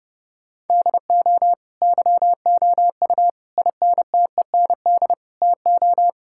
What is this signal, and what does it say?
Morse code, heard as a keyed tone, about 32 wpm, 700 Hz. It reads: DO YOU INTEND TO
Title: Sentences from Top 1600 Words 30wpm